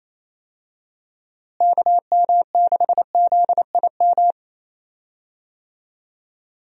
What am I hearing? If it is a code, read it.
Morse code, heard as a keyed tone, about 27 wpm, 700 Hz. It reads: KM6ZIM
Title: USA Call Signs - 6 Characters - Repeated 2x - 28wpm